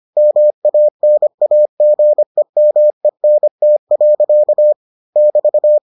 Morse code, heard as a keyed tone, about 25 wpm, 600 Hz. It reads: MANAGEMENT. <BT>